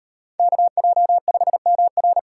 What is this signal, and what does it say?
Morse code, heard as a keyed tone, about 38 wpm, 700 Hz. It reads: KJ5MR